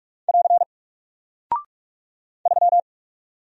Morse code, heard as a keyed tone, about 46 wpm, 700 Hz. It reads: <AR> 3
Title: Single Character (Letter:Number:Punctuation:Prosign) Alt 45wpm